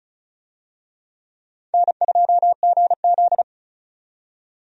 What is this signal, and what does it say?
Morse code, heard as a keyed tone, about 34 wpm, 700 Hz. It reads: N2GZ
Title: USA Call Signs - 4 Characters - 35wpm